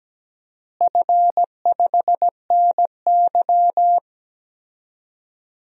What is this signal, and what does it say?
Morse code, heard as a keyed tone, about 17 wpm, 700 Hz. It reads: F5NY
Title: DXCC-100 Call Signs - 4 Characters - Repeated 2x - 17wpm